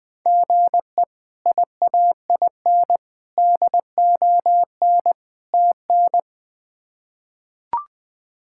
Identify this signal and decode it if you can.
Morse code, heard as a keyed tone, about 20 wpm, 700 Hz. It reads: GE IAIN DON TN
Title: K1USN SST QSO Practice - 20wpm